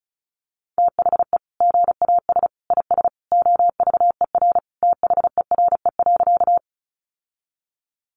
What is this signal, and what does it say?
Morse code, heard as a keyed tone, about 34 wpm, 700 Hz. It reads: THE GAS IS OVER THERE.
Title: Sentences from Top 1100 Words 35wpm